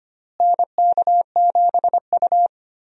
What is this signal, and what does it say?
Morse code, heard as a keyed tone, about 25 wpm, 700 Hz. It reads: NK7U
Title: USA Call Signs - 4 Characters - Repeated 3x - 25wpm